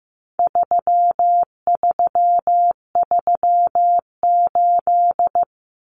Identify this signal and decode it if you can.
Morse code, heard as a keyed tone, about 15 wpm, 700 Hz. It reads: 3338
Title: Common Cut Numbers - 4-Digit 15wpm